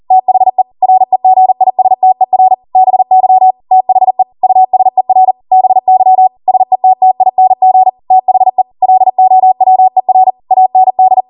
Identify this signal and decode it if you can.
Morse code, heard as a keyed tone, about 40 wpm, 750 Hz. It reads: THE REGISTER BY THE USER BY SETTING THE LOWER AND
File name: Morse / 40WPM